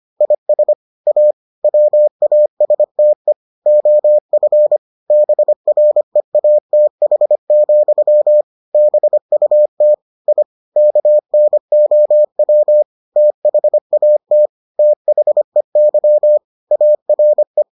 Morse code, heard as a keyed tone, about 25 wpm, 600 Hz. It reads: IS A WASTE OF BREATH, BUT I KNOW THAT THEY ARE